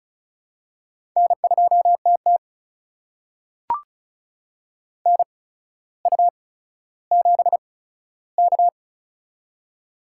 Morse code, heard as a keyed tone, about 34 wpm, spaced as for 15 wpm, 700 Hz. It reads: N2TT NU7K
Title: USA Call Signs - 8x Character Spacing - 35wpm